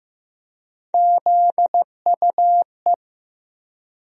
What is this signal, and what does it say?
Morse code, heard as a keyed tone, about 15 wpm, 700 Hz. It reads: ZUE
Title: USA Call Sign Postfixes - 3 Letters 15wpm